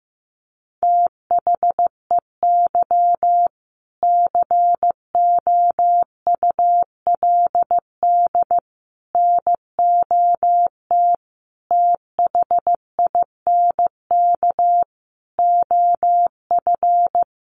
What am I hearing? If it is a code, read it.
Morse code, heard as a keyed tone, about 15 wpm, 700 Hz. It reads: THEY COULD NOT THINK OF
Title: Sentences from Top 1000 Words 15wpm